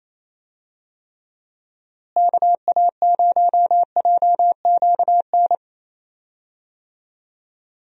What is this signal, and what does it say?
Morse code, heard as a keyed tone, about 28 wpm, 700 Hz. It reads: KA0JQN
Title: USA Call Signs - 6 Characters - Repeated 3x - 28wpm